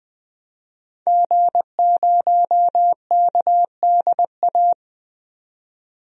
Morse code, heard as a keyed tone, about 21 wpm, 700 Hz. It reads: G0KDA
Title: DXCC-100 Call Signs - 5 Characters - 20wpm